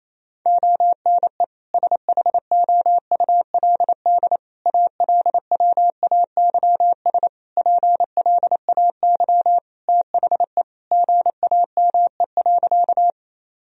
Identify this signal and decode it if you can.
Morse code, heard as a keyed tone, about 27 wpm, 700 Hz. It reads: ONE SHOULD ALWAYS PLAY THE GAME.
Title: Sentences from Top 1000 Words - Review Part 3 - 28wpm